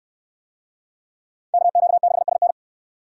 Morse code, heard as a keyed tone, about 66 wpm, 700 Hz. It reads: KQ6SN